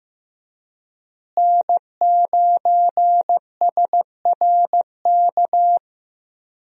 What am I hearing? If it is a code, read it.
Morse code, heard as a keyed tone, about 15 wpm, 700 Hz. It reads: N9SRK
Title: USA Call Signs - 5 Characters - Repeated 2x - 15wpm